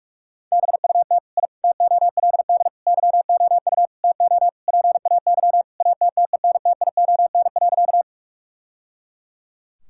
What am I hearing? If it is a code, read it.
Morse code, heard as a keyed tone, about 42 wpm, 700 Hz. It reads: BUT I TOLD YOU TO PAY ATTENTION.
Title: Sentences from Top 1000 Words - Review Part 6 - 45wpm